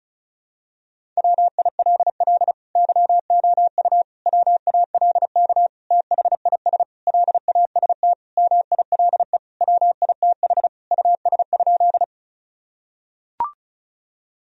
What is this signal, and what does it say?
Morse code, heard as a keyed tone, about 36 wpm, 700 Hz. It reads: WILL YOU WALK THIS LAST MILE WITH US?
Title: Sentences from Top 1000 Words - Review Part 1 - 35wpm